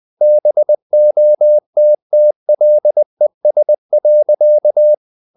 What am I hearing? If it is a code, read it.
Morse code, heard as a keyed tone, about 20 wpm, 600 Hz. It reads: BOTTLES.
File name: the old man and the sea / the old man and the sea part 1 20wpm